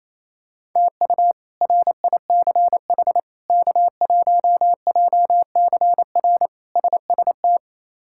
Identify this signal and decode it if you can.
Morse code, heard as a keyed tone, about 28 wpm, 700 Hz. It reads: TU RICH K1JCR SST
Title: K1USN SST QSO Practice - 28wpm